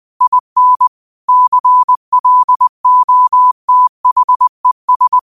Morse code, heard as a keyed tone, about 20 wpm, 1000 Hz. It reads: IN CLOTHES
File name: 20wpm